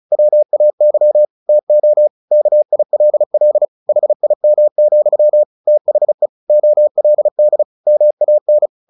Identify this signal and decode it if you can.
Morse code, heard as a keyed tone, about 34 wpm, 600 Hz. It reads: WAY TO KILL HIM, THE OLD MAN
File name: the old man and the sea 2 35wpm